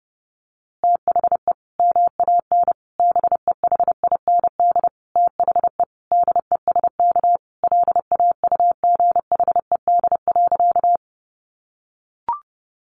Morse code, heard as a keyed tone, about 30 wpm, 700 Hz. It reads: THE MAN BEHIND THE DESK LAUGHED.
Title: A Sound Of Thunder 30wpm